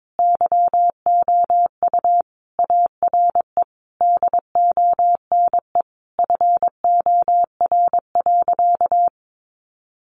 Morse code, heard as a keyed tone, about 22 wpm, 700 Hz. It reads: YOU ARE DONE FOR.
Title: Sentences from Top 1500 Words 22wpm